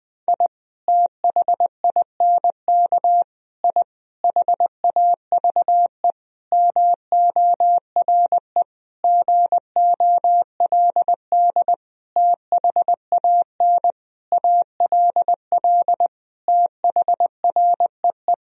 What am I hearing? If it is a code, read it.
Morse code, heard as a keyed tone, about 20 wpm, 700 Hz. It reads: I THINK I HAVE MORE GOLD THAN ALL THREE
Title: Sentences from Top 1000 Words - Review Part 3 - 20wpm